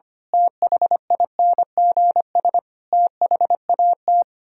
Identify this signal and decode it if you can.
Morse code, heard as a keyed tone, about 24 wpm, 700 Hz. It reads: THINGS THAT
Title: Speed-Racing - Sets of 5 Words 25wpm